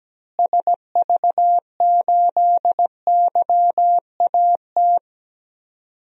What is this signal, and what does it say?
Morse code, heard as a keyed tone, about 17 wpm, 700 Hz. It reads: SV8YAT
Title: DXCC-100 Call Signs - 6 Characters - Repeated 2x - 17wpm